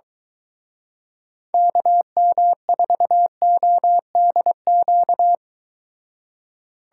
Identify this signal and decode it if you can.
Morse code, heard as a keyed tone, about 23 wpm, 700 Hz. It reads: KM4ODQ